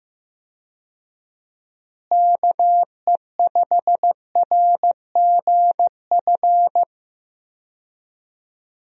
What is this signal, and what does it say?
Morse code, heard as a keyed tone, about 15 wpm, 700 Hz. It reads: KE5RGF
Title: USA Call Signs - 6 Characters - Repeated 2x - 15wpm